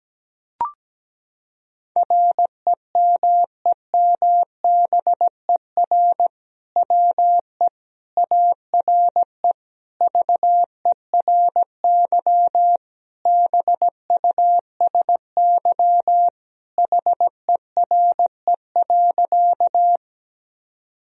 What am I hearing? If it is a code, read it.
Morse code, heard as a keyed tone, about 17 wpm, 700 Hz. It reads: REMEMBER WE ARE VERY BUSY HERE.
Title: Sentences from Top 1000 Words - Review Part 3 - 17wpm